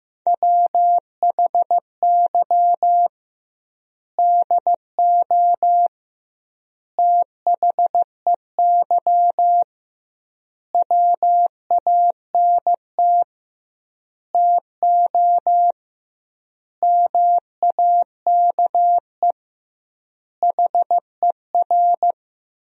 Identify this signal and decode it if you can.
Morse code, heard as a keyed tone, about 15 wpm, 700 Hz. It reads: WHY DO THEY WANT TO MAKE HER
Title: 2x Word-Spacing - Sentences from Top 400 Words - 15wpm